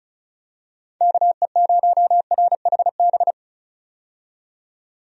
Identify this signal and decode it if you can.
Morse code, heard as a keyed tone, about 34 wpm, 700 Hz. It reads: KE0RHB
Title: USA Call Signs - 6 Characters - 35wpm